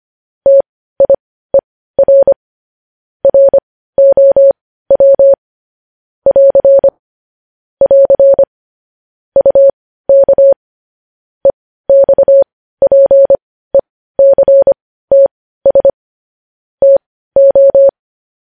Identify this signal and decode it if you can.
Morse code, heard as a keyed tone, about 25 wpm, spaced as for 15 wpm, 550 Hz. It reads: TIER ROW <AR> <AR> UK EXPECTS TO
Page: News Headlines in Morse Code at 15 WPM